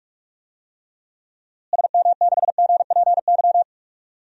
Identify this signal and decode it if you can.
Morse code, heard as a keyed tone, about 46 wpm, 700 Hz. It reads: SM6GPY
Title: DXCC-100 Call Signs - 6 Characters - Repeated 3x - 45wpm